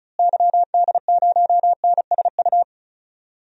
Morse code, heard as a keyed tone, about 36 wpm, 700 Hz. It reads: YD0NSU
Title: DXCC-100 Call Signs - 6 Characters - Repeated 3x - 35wpm